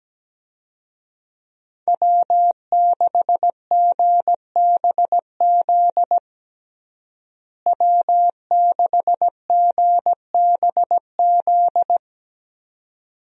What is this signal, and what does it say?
Morse code, heard as a keyed tone, about 17 wpm, 700 Hz. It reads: W6GBZ W6GBZ
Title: USA Call Signs - 5 Characters - Repeated 2x - 17wpm